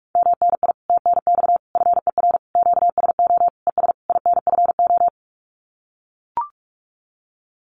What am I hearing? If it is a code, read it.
Morse code, heard as a keyed tone, about 44 wpm, 700 Hz. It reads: MNI TNX FER QSO ES INFO
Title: Common QSO Phrases Encore 45wpm